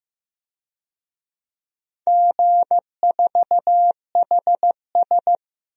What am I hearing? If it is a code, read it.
Morse code, heard as a keyed tone, about 15 wpm, 700 Hz. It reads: G4HS
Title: DXCC-100 Call Signs - 4 Characters - 15wpm